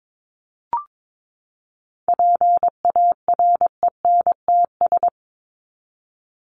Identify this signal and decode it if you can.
Morse code, heard as a keyed tone, about 22 wpm, 700 Hz. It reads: PARENTS
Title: ES:S Suffix - Top 100 Words with Base Words 22wpm